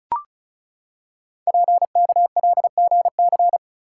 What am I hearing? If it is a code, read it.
Morse code, heard as a keyed tone, about 36 wpm, 690 Hz. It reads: PKLGC
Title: Code Groups - 5 Letter:Numbers - 35wpm